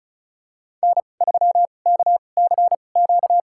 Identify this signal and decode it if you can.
Morse code, heard as a keyed tone, about 34 wpm, spaced as for 25 wpm, 700 Hz. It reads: N3KCQ
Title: USA Call Signs - 5 Characters - 2x Character Spacing - 35wpm